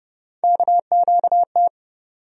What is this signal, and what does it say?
Morse code, heard as a keyed tone, about 30 wpm, 700 Hz. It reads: KQT
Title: USA Call Sign Postfixes - 3 Letters 30wpm